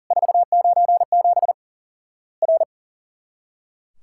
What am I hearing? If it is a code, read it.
Morse code, heard as a keyed tone, about 40 wpm, 700 Hz. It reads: 497 R
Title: Worked All Europe DX Contest - QTC Exchange - 40wpm